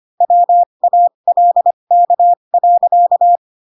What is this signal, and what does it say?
Morse code, heard as a keyed tone, about 25 wpm, spaced as for 22 wpm, 700 Hz. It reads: WALK.
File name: Pinocchio / Pinocchio ch3 18 25 ik1hge